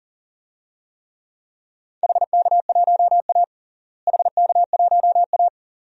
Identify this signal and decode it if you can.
Morse code, heard as a keyed tone, about 40 wpm, 700 Hz. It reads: HK1A HK1A
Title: DXCC-100 Call Signs - 4 Characters - Repeated 2x - 40wpm